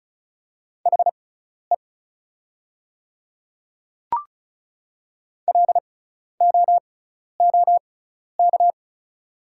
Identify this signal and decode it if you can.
Morse code, heard as a keyed tone, about 36 wpm, spaced as for 12 wpm, 700 Hz. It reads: HE LOOK
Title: 6x Character Spacing - Top 100 Words - 35wpm